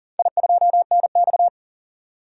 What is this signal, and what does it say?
Morse code, heard as a keyed tone, about 37 wpm, 700 Hz. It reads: I2NX
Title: DXCC-100 Call Signs - 4 Characters - 40wpm